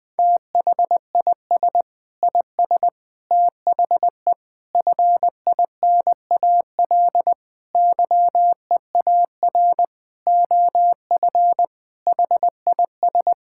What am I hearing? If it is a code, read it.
Morse code, heard as a keyed tone, about 20 wpm, 700 Hz. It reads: THIS IS THE FINAL YEAR OF HIS
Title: Sentences with 6-Word Phrases - 20wpm